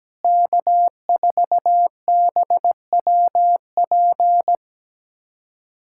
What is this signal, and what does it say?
Morse code, heard as a keyed tone, about 17 wpm, 700 Hz. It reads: K4BWP
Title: K1USN SST QSO Practice - 17wpm